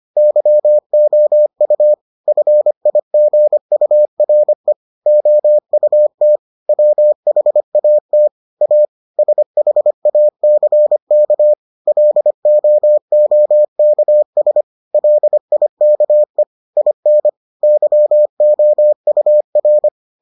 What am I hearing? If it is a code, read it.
Morse code, heard as a keyed tone, about 25 wpm, 600 Hz. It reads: YOU FIGURE OUT WHAT A SHACK LOOKS LIKE IN YOUR